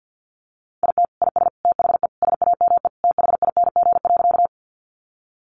Morse code, heard as a keyed tone, about 50 wpm, 700 Hz. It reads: IT IS THE SAME THING.
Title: Sentences from Top 200 Words 50wpm